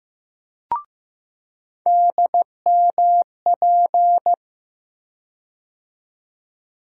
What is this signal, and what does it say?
Morse code, heard as a keyed tone, about 15 wpm, 700 Hz. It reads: DMP
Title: USA Call Sign Postfixes - 3 Letters 15wpm